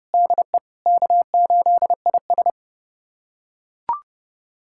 Morse code, heard as a keyed tone, about 30 wpm, 700 Hz. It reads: DE K8IS
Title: USA Callsign DE Callsign - 4 Characters - 30wpm